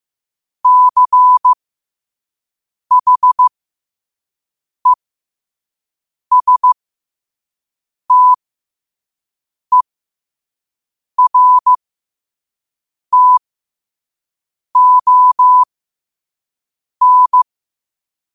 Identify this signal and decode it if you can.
Morse code, heard as a keyed tone, about 15 wpm, spaced as for 5 wpm, 1000 Hz. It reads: CHESTERTON
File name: 5wpm